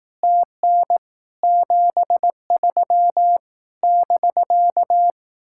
Morse code, heard as a keyed tone, about 18 wpm, 700 Hz. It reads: TN 73 <BK>
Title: Speed-Racing Common QSO Phrases Encore - 15wpm